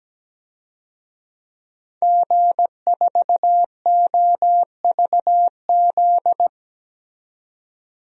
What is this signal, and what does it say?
Morse code, heard as a keyed tone, about 17 wpm, 700 Hz. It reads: G4OVZ
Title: DXCC-100 Call Signs - 5 Characters - 17wpm